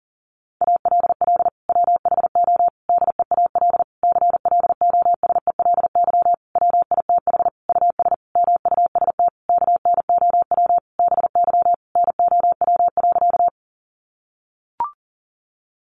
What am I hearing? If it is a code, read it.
Morse code, heard as a keyed tone, about 40 wpm, 700 Hz. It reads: ALL WHO DEAL CLOSELY WITH US MUST KNOW BY NOW.
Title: Sentences from Top 1400 Words 40wpm